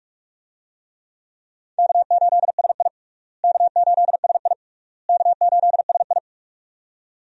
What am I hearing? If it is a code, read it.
Morse code, heard as a keyed tone, about 41 wpm, 700 Hz. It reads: K8SI K8SI K8SI
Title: USA Call Signs - 4 Characters - Repeated 3x - 45wpm